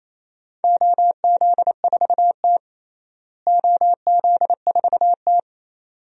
Morse code, heard as a keyed tone, about 28 wpm, 700 Hz. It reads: OZ4T OZ4T
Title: DXCC-100 Call Signs - 4 Characters - Repeated 2x - 28wpm